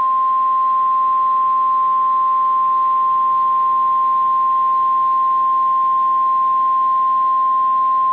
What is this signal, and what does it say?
Emergency Alert System (SAME) tones (the 1050 Hz attention tone). It is heard from 0.7 to 7.7 s.